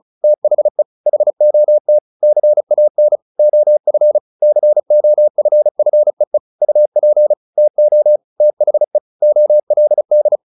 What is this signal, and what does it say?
Morse code, heard as a keyed tone, about 34 wpm, 600 Hz. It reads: THE HOT CAN OF COFFEE UP TO THE OLD
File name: the old man and the sea 2 35wpm